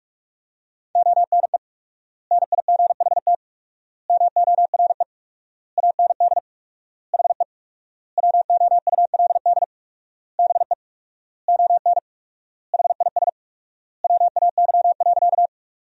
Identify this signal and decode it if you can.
Morse code, heard as a keyed tone, about 46 wpm, 700 Hz. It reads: ONE NIGHT MORE AND HE WOULD BE ON HIS WAY.
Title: 4x Word-Spacing - Sentences from Top 300 Words - 45wpm